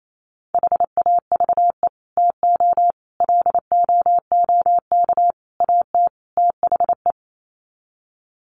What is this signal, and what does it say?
Morse code, heard as a keyed tone, about 28 wpm, 700 Hz. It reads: HAVE TO LOOK AT THE
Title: Sets of 5 Words Encore 28wpm